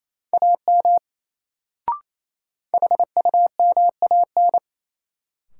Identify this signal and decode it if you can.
Morse code, heard as a keyed tone, about 28 wpm, 700 Hz. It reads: AM HUMAN
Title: Sentences from Top 600 Words - New Word Drill - 28wpm